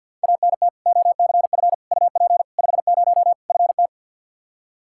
Morse code, heard as a keyed tone, about 47 wpm, 700 Hz. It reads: ANT OCF UP 50 FT
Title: Practice QSO - Exchange 2 - Info - 50wpm